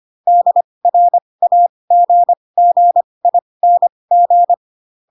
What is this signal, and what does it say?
Morse code, heard as a keyed tone, about 25 wpm, spaced as for 20 wpm, 700 Hz. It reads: DRAGGING